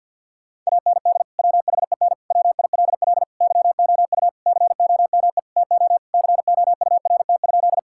Morse code, heard as a keyed tone, about 50 wpm, 700 Hz. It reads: AND WHEN WILL YOU COME TO COURT?
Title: Sentences from Top 1600 Words 50wpm